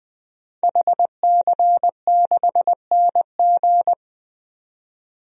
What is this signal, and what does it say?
Morse code, heard as a keyed tone, about 20 wpm, 700 Hz. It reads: HC6NG
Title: DXCC-100 Call Signs - 5 Characters - 20wpm